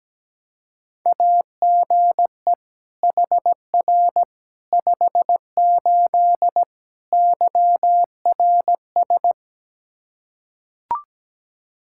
Morse code, heard as a keyed tone, about 17 wpm, 700 Hz. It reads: AGE HR 58 YRS
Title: Practice QSO - Exchange 3 - More Info - 17wpm